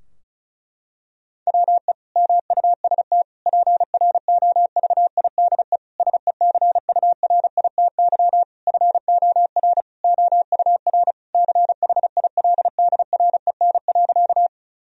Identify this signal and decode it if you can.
Morse code, heard as a keyed tone, about 36 wpm, 700 Hz. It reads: WE MUST PROVIDE SECURITY FOR OUR CHILDREN.
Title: Sentences from Top 1000 Words - Review Part 4 - 35wpm